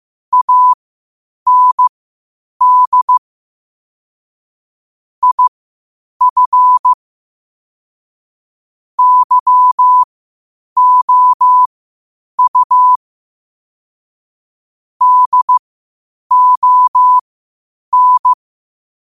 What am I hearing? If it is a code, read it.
Morse code, heard as a keyed tone, about 15 wpm, spaced as for 8 wpm, 1000 Hz. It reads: AND IF YOU DON
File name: qotdfiles / 8wpm